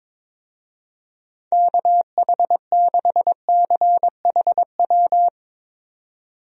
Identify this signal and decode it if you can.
Morse code, heard as a keyed tone, about 22 wpm, 700 Hz. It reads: KH6CHW
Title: USA Call Signs - 6 Characters - 22wpm